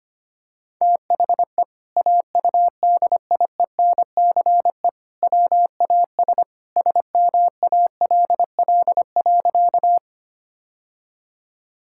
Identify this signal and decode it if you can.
Morse code, heard as a keyed tone, about 25 wpm, 700 Hz. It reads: THE AUDIENCE WAS SMALL.
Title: Sentences from Top 1000 Words - Review Part 5 - 25wpm